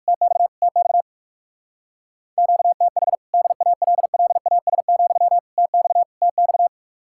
Morse code, heard as a keyed tone, about 57 wpm, 700 Hz. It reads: TX TX QTH DALLAS, TX TX